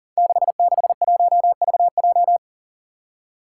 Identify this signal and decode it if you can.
Morse code, heard as a keyed tone, about 43 wpm, 700 Hz. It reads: 661VJ
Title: Five Challenging Letter-Number Combinations 40wpm